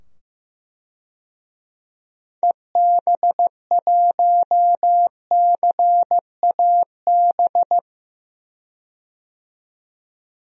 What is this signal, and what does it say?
Morse code, heard as a keyed tone, about 15 wpm, 700 Hz. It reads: EB1CAB
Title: DXCC-100 Call Signs - 6 Characters - 15wpm